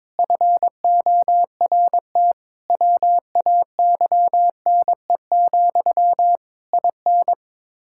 Morse code, heard as a keyed tone, about 22 wpm, 700 Hz. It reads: FORT WAYNE, IN
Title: Practice QSO - Exchange 1 - Beginning - 22wpm